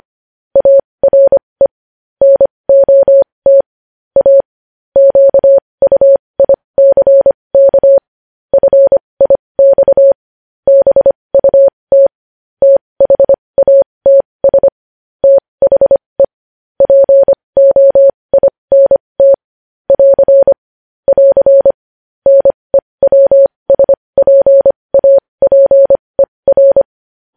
Morse code, heard as a keyed tone, about 25 wpm, spaced as for 20 wpm, 550 Hz. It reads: ARE NOT A QUICK FIX BUT THATS THE POINT <AR> <AR> NEWSPAPER
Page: News Headlines in Morse Code at 20 WPM